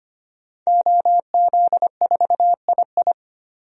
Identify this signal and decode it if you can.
Morse code, heard as a keyed tone, about 25 wpm, 700 Hz. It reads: OZ4II
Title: DXCC-100 Call Signs - 5 Characters - 25wpm